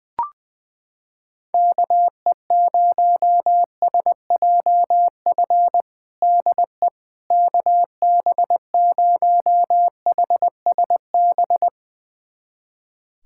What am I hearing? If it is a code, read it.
Morse code, heard as a keyed tone, about 21 wpm, 700 Hz. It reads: KE0SJF DE KB0HSB
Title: USA Callsign DE Callsign - 6 Characters - 20wpm